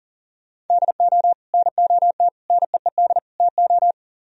Morse code, heard as a keyed tone, about 49 wpm, 700 Hz. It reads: DO NOT NEED TO
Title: Speed-Racing - Sets of 4 Words 40wpm